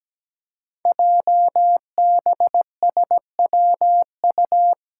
Morse code, heard as a keyed tone, about 17 wpm, 690 Hz. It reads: JBSWU
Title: Code Groups - 5 Letter:Numbers - 17wpm